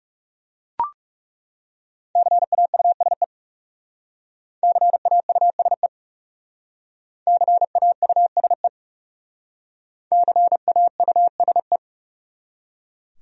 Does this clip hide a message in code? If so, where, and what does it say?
Yes: Morse code from 2.2 to 11.8 s, heard as a keyed tone, about 39 wpm, 700 Hz. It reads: CAUSE CAUSE CAUSE CAUSE